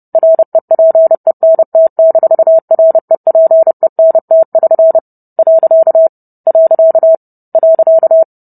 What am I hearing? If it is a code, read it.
Morse code, heard as a keyed tone, about 30 wpm, 650 Hz. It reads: REPENT-REPENT<SN> . .